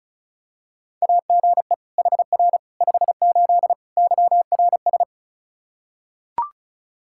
Morse code, heard as a keyed tone, about 36 wpm, 700 Hz. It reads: AGE HR 58 YRS
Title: Practice QSO - Exchange 3 - More Info - 35wpm